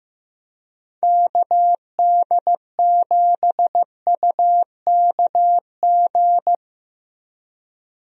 Morse code, heard as a keyed tone, about 16 wpm, 700 Hz. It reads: KD7UKG
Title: USA Call Signs - 6 Characters - Repeated 2x - 15wpm